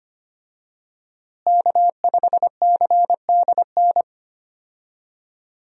Morse code, heard as a keyed tone, about 25 wpm, 700 Hz. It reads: K5CDN
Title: USA Call Signs - 5 Characters - Repeated 3x - 25wpm